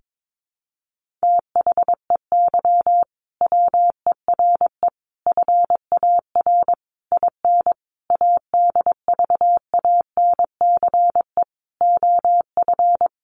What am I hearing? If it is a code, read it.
Morse code, heard as a keyed tone, about 22 wpm, 700 Hz. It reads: THEY WERE FAR IN ADVANCE OF
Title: Sentences from Top 1400 Words 22wpm